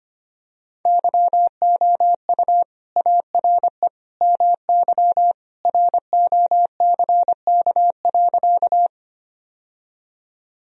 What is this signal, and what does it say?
Morse code, heard as a keyed tone, about 25 wpm, 700 Hz. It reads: YOU ARE MY ROCK.
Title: Sentences from Top 1000 Words - Review Part 2 - 25wpm